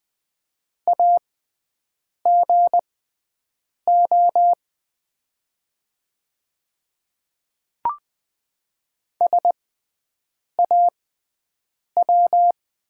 Morse code, heard as a keyed tone, about 20 wpm, spaced as for 7 wpm, 700 Hz. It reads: AGO SAW
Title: 6x Character Spacing - Top 500 Words - 20wpm